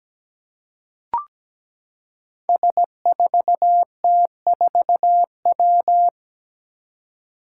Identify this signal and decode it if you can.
Morse code, heard as a keyed tone, about 17 wpm, 700 Hz. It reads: S4T4W
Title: Five Challenging Letter-Number Combinations 17wpm